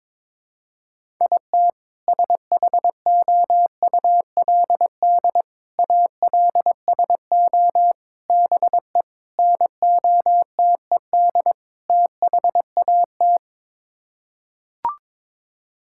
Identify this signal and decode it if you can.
Morse code, heard as a keyed tone, about 22 wpm, 700 Hz. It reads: IT SHOULD ALSO BE NOTED THAT
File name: practice / 22wpm / Sets of 6 Words - 22wpm